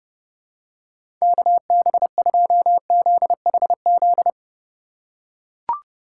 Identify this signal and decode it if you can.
Morse code, heard as a keyed tone, about 30 wpm, 700 Hz. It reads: KB2ZHZ